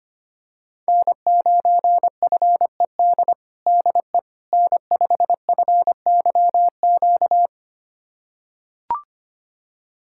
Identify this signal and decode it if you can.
Morse code, heard as a keyed tone, about 25 wpm, 700 Hz. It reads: N9FED DE N5FYQ
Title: USA Callsign DE Callsign - 5 Characters - 25wpm